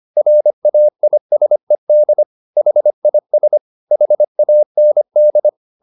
Morse code, heard as a keyed tone, about 25 wpm, 600 Hz. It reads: RAISED HIS HAND